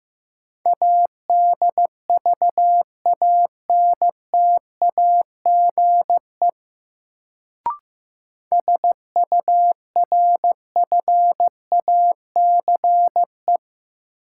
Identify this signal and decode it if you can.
Morse code, heard as a keyed tone, about 15 wpm, 700 Hz. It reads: ADVANTAGE SURFACE
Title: Sets of 4 Words - New Word Drill - 15wpm